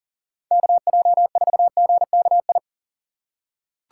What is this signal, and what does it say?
Morse code, heard as a keyed tone, about 40 wpm, 700 Hz. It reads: KJ4GKI
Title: USA Call Signs - 6 Characters - 3x Character Spacing - 40wpm